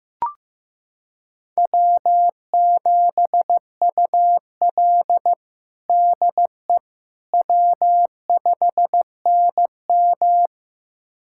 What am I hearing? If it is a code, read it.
Morse code, heard as a keyed tone, about 15 wpm, 700 Hz. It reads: W7UL DE W5NM
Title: USA Callsign DE Callsign - 4 Characters - 15wpm